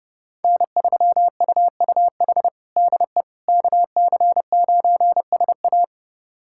Morse code, heard as a keyed tone, about 32 wpm, 700 Hz. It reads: N3UUH DE KC9SA
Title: USA Callsign DE Callsign - 5 Characters - 30wpm